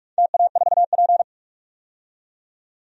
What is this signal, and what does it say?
Morse code, heard as a keyed tone, about 43 wpm, 700 Hz. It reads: TA4P
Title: DXCC-100 Call Signs - 4 Characters - 45wpm